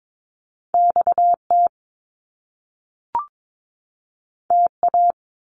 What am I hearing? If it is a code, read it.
Morse code, heard as a keyed tone, about 22 wpm, 700 Hz. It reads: XT TA
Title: Top Two Letter Combinations in Words 22wpm